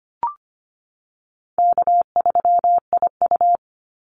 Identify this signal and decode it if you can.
Morse code, heard as a keyed tone, about 25 wpm, 700 Hz. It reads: K3IU
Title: Easy Call Signs - 25wpm